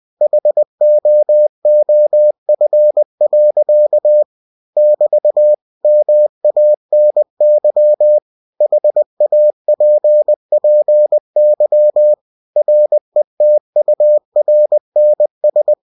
Morse code, heard as a keyed tone, about 20 wpm, 600 Hz. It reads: HOOF. <BT> MANY HAPPY RETURNS
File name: cw-books / winnie-the-pooh / winnie-20wpm-page-0054